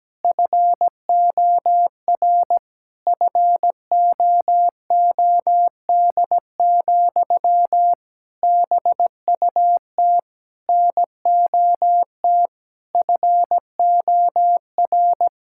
Morse code, heard as a keyed tone, about 17 wpm, 700 Hz. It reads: FOR FOOD, BUT NOT FOR
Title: Sentences from Top 1600 Words 17wpm